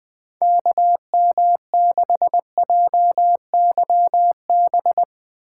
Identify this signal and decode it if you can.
Morse code, heard as a keyed tone, about 19 wpm, 700 Hz. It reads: KM6JYB
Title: Speed Racing - USA 2x3 Call Signs 20wpm